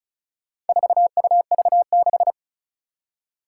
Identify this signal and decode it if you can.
Morse code, heard as a keyed tone, about 36 wpm, 700 Hz. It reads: 4UV6
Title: Four Challenging Letter-Number Combinations 35wpm